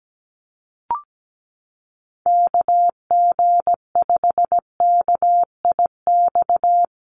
Morse code, heard as a keyed tone, about 17 wpm, 700 Hz. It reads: KG5KIX
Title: USA Call Signs Repeated 2x 17wpm